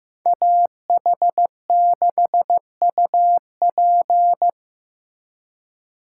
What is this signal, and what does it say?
Morse code, heard as a keyed tone, about 18 wpm, 700 Hz. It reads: AH6UP
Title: Speed Racing - USA 2x2 Call Signs 15wpm